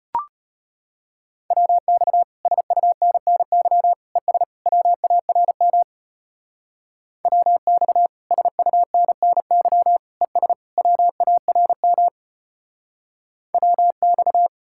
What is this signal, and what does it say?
Morse code, heard as a keyed tone, about 31 wpm, 700 Hz. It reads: WX SUNNY ES WARM WX SUNNY ES WARM WX